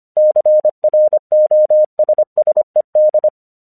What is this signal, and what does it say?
Morse code, heard as a keyed tone, about 25 wpm, 610 Hz. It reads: CROSSED
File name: 1 English 25wpm / AQC23